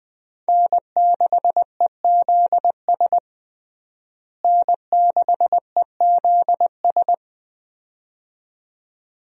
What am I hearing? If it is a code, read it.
Morse code, heard as a keyed tone, about 20 wpm, 700 Hz. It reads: N6EZS N6EZS
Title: USA Call Signs - 5 Characters - Repeated 2x - 20wpm